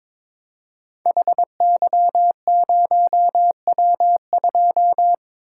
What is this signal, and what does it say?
Morse code, heard as a keyed tone, about 23 wpm, 700 Hz. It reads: HY0W2
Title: Code Groups - 5 Letter:Numbers - 22wpm